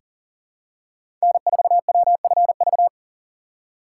Morse code, heard as a keyed tone, about 40 wpm, 700 Hz. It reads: N4WFV
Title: USA Call Signs - 5 Characters - 3x Character Spacing - 40wpm